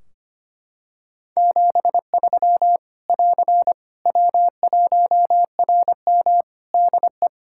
Morse code, heard as a keyed tone, about 25 wpm, 700 Hz. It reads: 73 <AR> W1RM DE
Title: Practice QSO - Exchange 4 - Ending - 25wpm